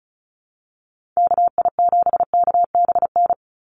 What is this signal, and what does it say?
Morse code, heard as a keyed tone, about 34 wpm, 700 Hz. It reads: KI7KBN
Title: USA Call Signs 35wpm